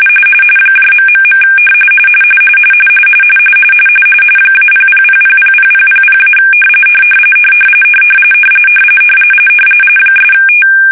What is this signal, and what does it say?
RTTY, about 45.45 baud. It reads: ... -000000000- '
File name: fsk-ascii850-pix / fish